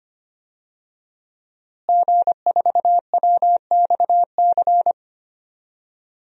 Morse code, heard as a keyed tone, about 25 wpm, 700 Hz. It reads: G4WXC
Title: DXCC-100 Call Signs - 5 Characters - Repeated 2x - 25wpm